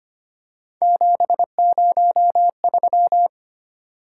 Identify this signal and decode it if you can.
Morse code, heard as a keyed tone, about 29 wpm, 700 Hz. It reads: 703
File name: Speed-Racing - Three Letter-Number Combinations - 25wpm